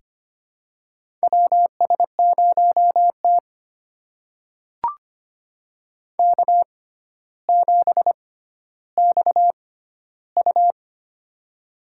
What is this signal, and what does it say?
Morse code, heard as a keyed tone, about 25 wpm, spaced as for 13 wpm, 700 Hz. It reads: WS0T K7XU
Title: USA Call Signs - 4 Characters - 6x Character Spacing - 25wpm